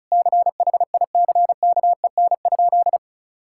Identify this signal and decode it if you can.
Morse code, heard as a keyed tone, about 34 wpm, 700 Hz. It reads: CHICKEN?
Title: Sentences from Top 1000 Words - Review Part 5 - 35wpm